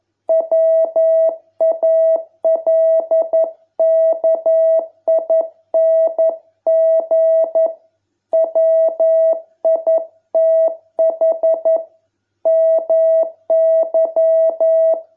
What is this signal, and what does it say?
Morse code, heard as a keyed tone, about 11 wpm, 660 Hz. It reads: WALKING WITH MY